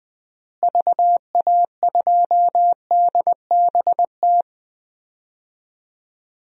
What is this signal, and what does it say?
Morse code, heard as a keyed tone, about 20 wpm, 700 Hz. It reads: VA2DBT
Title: DXCC-100 Call Signs - 6 Characters - Repeated 2x - 20wpm